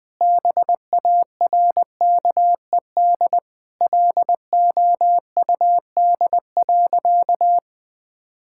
Morse code, heard as a keyed tone, about 20 wpm, 700 Hz. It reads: BARKED LOUD.
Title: Sentences with 2-Word Phrases - Spoken Beginning - 20wpm